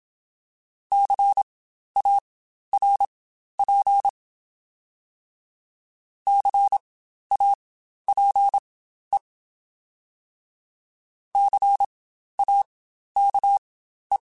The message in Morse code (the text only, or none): CARP CAPE CAKE